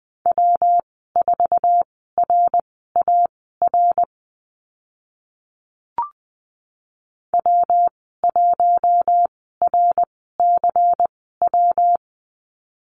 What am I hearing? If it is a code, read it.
Morse code, heard as a keyed tone, about 20 wpm, spaced as for 15 wpm, 700 Hz. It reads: W4RAR W1RCW
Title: 128 Lesson 21g - Number 6 - Call Signs 20wpm